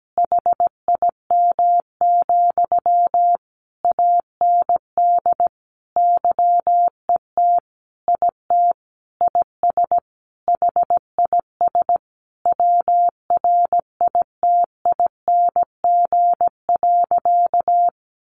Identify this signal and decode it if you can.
Morse code, heard as a keyed tone, about 17 wpm, 700 Hz. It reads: HIM, AND YET IT IS HIS WRITING.
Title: Sentences from Top 800 Words 17wpm